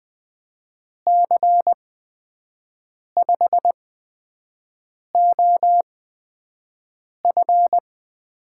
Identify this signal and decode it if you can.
Morse code, heard as a keyed tone, about 20 wpm, spaced as for 5 wpm, 700 Hz. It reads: C5OF
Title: USA Call Signs - 8x Character Spacing - 20wpm